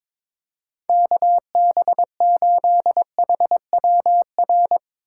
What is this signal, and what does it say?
Morse code, heard as a keyed tone, about 22 wpm, 700 Hz. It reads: KB8HWR
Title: USA Call Signs - 6 Characters - 4x Character Spacing - 22wpm